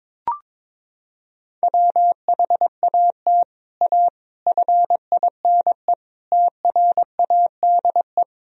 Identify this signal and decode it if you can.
Morse code, heard as a keyed tone, about 22 wpm, 700 Hz. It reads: WHAT A FINE TRADE
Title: Sentences from Top 1000 Words - Review Part 4 - 22wpm